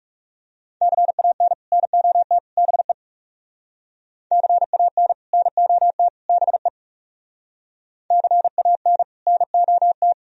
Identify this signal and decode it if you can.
Morse code, heard as a keyed tone, about 36 wpm, 700 Hz. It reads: CAN NOT BE CAN NOT BE CAN NOT
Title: Speed-Racing - Sets of 3 Words 30wpm